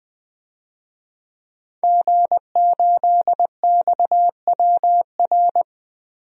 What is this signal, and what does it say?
Morse code, heard as a keyed tone, about 20 wpm, 700 Hz. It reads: G8XWR
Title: DXCC-100 Call Signs - 5 Characters - 20wpm